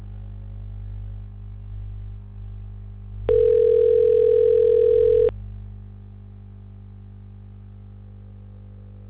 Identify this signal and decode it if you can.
Ringback tone (North America), 440 + 480 Hz.
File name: recordings / RE31ebba49cbc2b4d38f8cf2f24df9b369